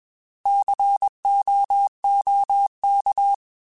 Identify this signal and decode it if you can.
Morse code, heard as a keyed tone, about 21 wpm, 800 Hz. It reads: COOK